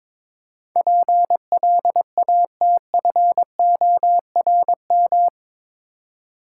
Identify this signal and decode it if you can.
Morse code, heard as a keyed tone, about 22 wpm, 700 Hz. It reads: PLATFORM
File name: practice / 22wpm / Eight Letter Words - 22wpm